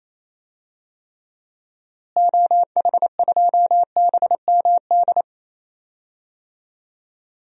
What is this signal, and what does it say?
Morse code, heard as a keyed tone, about 28 wpm, 700 Hz. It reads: OH2BMD
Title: DXCC-100 Call Signs - 6 Characters - Repeated 3x - 28wpm